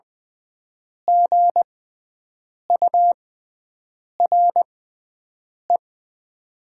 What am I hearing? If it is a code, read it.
Morse code, heard as a keyed tone, about 20 wpm, spaced as for 7 wpm, 700 Hz. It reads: GURE
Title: 6x Character Spacing - Top 500 Words - 20wpm